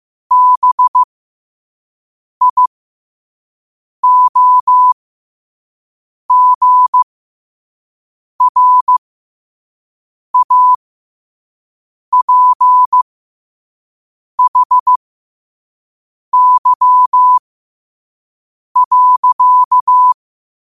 Morse code, heard as a keyed tone, about 15 wpm, spaced as for 5 wpm, 1000 Hz. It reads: BIOGRAPHY.
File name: qotdfiles / 5wpm